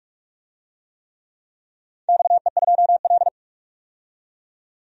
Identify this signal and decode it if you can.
Morse code, heard as a keyed tone, about 44 wpm, 700 Hz. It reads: XE2L
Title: DXCC-100 Call Signs - 4 Characters - 45wpm